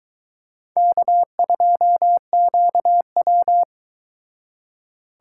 Morse code, heard as a keyed tone, about 22 wpm, 700 Hz. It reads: K2QW